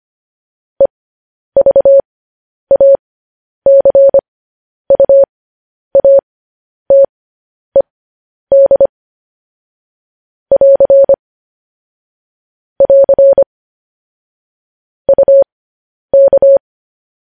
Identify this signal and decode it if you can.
Morse code, heard as a keyed tone, about 25 wpm, spaced as for 10 wpm, 550 Hz. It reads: EVACUATED <AR> <AR> UK
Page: news headlines in morse code at 10 wpm